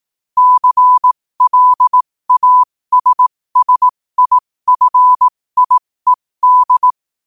Morse code, heard as a keyed tone, about 18 wpm, spaced as for 15 wpm, 1000 Hz. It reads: CLASSIFIED
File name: qotdfiles / 15wpm